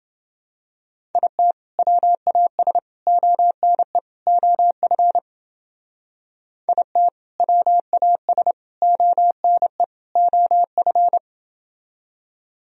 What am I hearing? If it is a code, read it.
Morse code, heard as a keyed tone, about 30 wpm, 700 Hz. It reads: IT WAS ONE OF IT WAS ONE OF